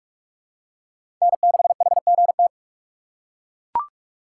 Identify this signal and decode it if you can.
Morse code, heard as a keyed tone, about 42 wpm, 700 Hz. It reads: N6HGT
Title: USA Call Signs - 6x Character Spacing - 45wpm